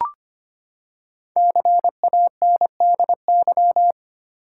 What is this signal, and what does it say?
Morse code, heard as a keyed tone, about 25 wpm, 700 Hz. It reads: CANDY
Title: Five Letter Words - 25wpm